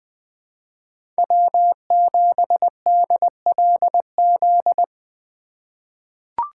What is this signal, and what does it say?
Morse code, heard as a keyed tone, about 21 wpm, 700 Hz. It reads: W7DLZ
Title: 2023 POTA Activator Call Signs - 20wpm